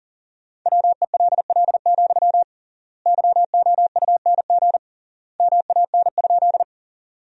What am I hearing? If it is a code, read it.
Morse code, heard as a keyed tone, about 40 wpm, 700 Hz. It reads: WELL, YOUNG MAN?
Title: 3x Word-Spacing - Sentences from Top 400 Words - 40wpm